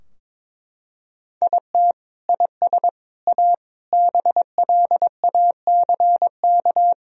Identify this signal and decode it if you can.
Morse code, heard as a keyed tone, about 22 wpm, 700 Hz. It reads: IT IS A BLACK
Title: Sentences from Top 1000 Words - Review Part 3 - 22wpm